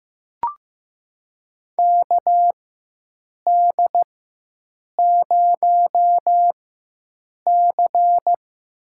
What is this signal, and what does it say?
Morse code, heard as a keyed tone, about 15 wpm, spaced as for 7 wpm, 700 Hz. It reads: KD0C
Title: USA Call Signs - 4x Character Spacing - 15wpm